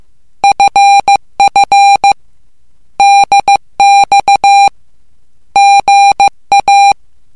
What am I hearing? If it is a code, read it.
Morse code, heard as a keyed tone, about 15 wpm, 800 Hz. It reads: FF DX GA
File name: morse / english